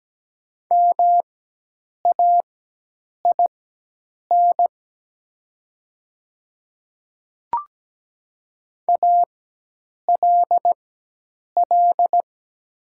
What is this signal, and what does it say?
Morse code, heard as a keyed tone, about 17 wpm, spaced as for 8 wpm, 700 Hz. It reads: MAIN ALL
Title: 4x Character Spacing - Top 500 Words - 17wpm